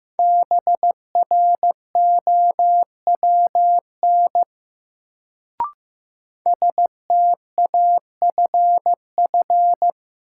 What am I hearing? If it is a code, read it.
Morse code, heard as a keyed tone, about 15 wpm, 700 Hz. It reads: BROWN STAFF
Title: Sentences from Top 1200 Words - New Word Drill Alt - 15wpm